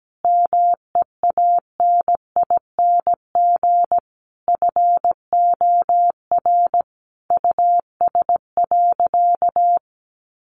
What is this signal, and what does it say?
Morse code, heard as a keyed tone, about 17 wpm, 700 Hz. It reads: MEANING FOR US.
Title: Sentences from Top 900 Words 17wpm